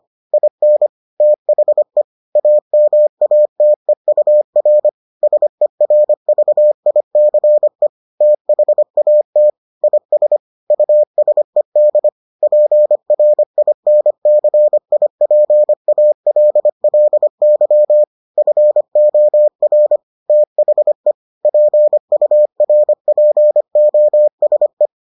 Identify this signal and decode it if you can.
Morse code, heard as a keyed tone, about 25 wpm, 600 Hz. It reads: IN THE AMATEUR SERVICE THAT IS USED PRINCIPALLY FOR THE PURPOSE